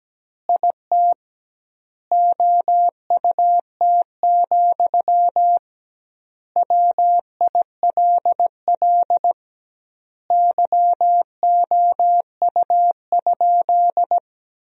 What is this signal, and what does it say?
Morse code, heard as a keyed tone, about 17 wpm, 700 Hz. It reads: IT OUT, WILL YOU?
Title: 2x Word-Spacing - Sentences from Top 300 Words - 17wpm